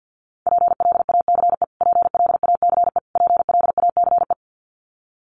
Morse code, heard as a keyed tone, about 50 wpm, 700 Hz. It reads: PLACE PLACE PLACE
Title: Top 500 Words Repeated 50wpm